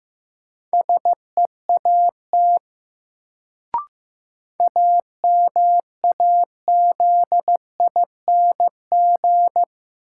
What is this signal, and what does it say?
Morse code, heard as a keyed tone, about 15 wpm, 700 Hz. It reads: SEAT AMAZING
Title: Sentences from Top 1500 Words - New Word Drill - 15wpm